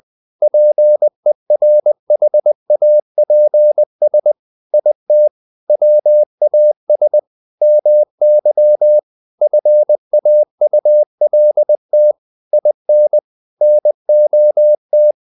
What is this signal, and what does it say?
Morse code, heard as a keyed tone, about 20 wpm, 600 Hz. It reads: PERHAPS IT WAS MY FAULT IN NOT
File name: the old man and the sea part 2 20wpm